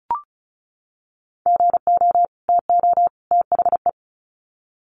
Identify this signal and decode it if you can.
Morse code, heard as a keyed tone, about 34 wpm, 700 Hz. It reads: GO TO THE
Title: Sets of 3 Words 35wpm